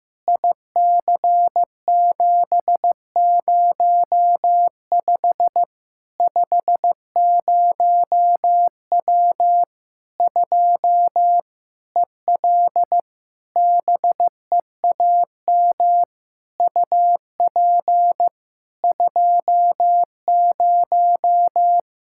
Morse code, heard as a keyed tone, about 16 wpm, 700 Hz. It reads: IC705 50W 2 EL BEAM UP 20
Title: Practice QSO - Exchange 2 - Info - 15wpm